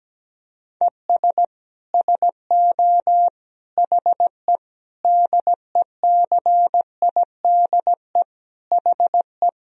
Morse code, heard as a keyed tone, about 17 wpm, 700 Hz. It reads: ES SO HE DECIDE HE
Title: Sentences from Top 500 Words and Abbreviations 17wpm